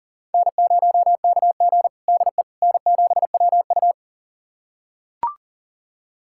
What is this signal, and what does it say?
Morse code, heard as a keyed tone, about 40 wpm, 700 Hz. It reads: N0KG DE N7WU
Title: USA Callsign DE Callsign - 4 Characters - 40wpm